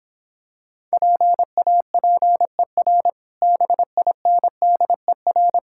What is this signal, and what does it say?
Morse code, heard as a keyed tone, about 26 wpm, 700 Hz. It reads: PAPER BINDER